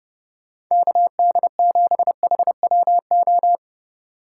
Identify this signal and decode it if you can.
Morse code, heard as a keyed tone, about 31 wpm, 700 Hz. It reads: KD7HWO